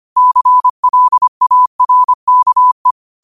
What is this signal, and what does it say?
Morse code, heard as a keyed tone, about 25 wpm, 1000 Hz. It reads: CLARKE
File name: qotdfiles / 25wpm